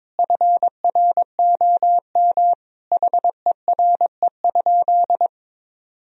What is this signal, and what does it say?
Morse code, heard as a keyed tone, about 22 wpm, 700 Hz. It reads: FROM HERE?
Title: Sentences from Top 1000 Words - Review Part 1 - 22wpm